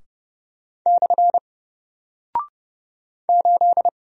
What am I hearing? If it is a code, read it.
Morse code, heard as a keyed tone, about 30 wpm, 700 Hz. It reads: / 8
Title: Single Character (Letter:Number:Punctuation:Prosign) 30wpm